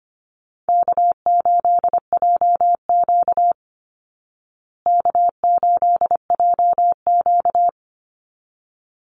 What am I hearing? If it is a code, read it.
Morse code, heard as a keyed tone, about 25 wpm, 700 Hz. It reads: K8JQ K8JQ
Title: Easy Call Signs Repeated 2x - 25wpm